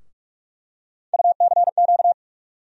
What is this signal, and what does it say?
Morse code, heard as a keyed tone, about 42 wpm, 700 Hz. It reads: UCQ
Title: USA Call Sign Postfixes - 3 Letters 45wpm